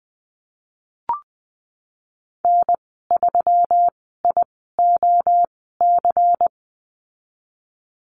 Morse code, heard as a keyed tone, about 19 wpm, spaced as for 14 wpm, 700 Hz. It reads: N3IOC
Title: 186 Lesson 30g - Letter b - Call Signs 20wpm